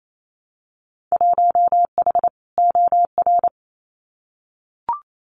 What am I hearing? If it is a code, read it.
Morse code, heard as a keyed tone, about 28 wpm, 700 Hz. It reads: 1H OR
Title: Winter Field Day Exchanges 28wpm